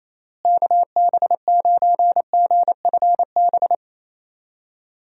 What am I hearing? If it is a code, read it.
Morse code, heard as a keyed tone, about 27 wpm, 700 Hz. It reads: KB9GFB
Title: USA Call Signs - 6 Characters - Repeated 2x - 28wpm